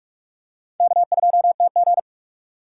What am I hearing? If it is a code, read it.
Morse code, heard as a keyed tone, about 46 wpm, 700 Hz. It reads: K2TG
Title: USA Call Signs - 4 Characters - 6x Character Spacing - 45wpm